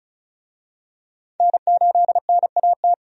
Morse code, heard as a keyed tone, about 36 wpm, 700 Hz. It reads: N8NAT
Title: USA Call Signs - 5 Characters - 35wpm